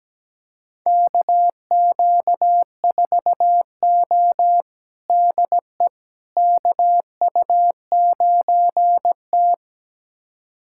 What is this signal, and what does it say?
Morse code, heard as a keyed tone, about 17 wpm, 700 Hz. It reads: KQ4O DE KU9T
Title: USA Callsign DE Callsign - 4 Characters - 17wpm